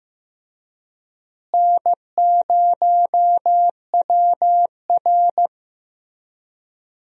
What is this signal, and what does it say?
Morse code, heard as a keyed tone, about 15 wpm, 700 Hz. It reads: N0WR
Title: USA Call Signs - 4 Characters - Repeated 2x - 15wpm